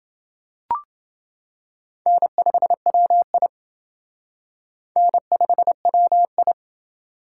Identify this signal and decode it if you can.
Morse code, heard as a keyed tone, about 28 wpm, 700 Hz. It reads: N5WI N5WI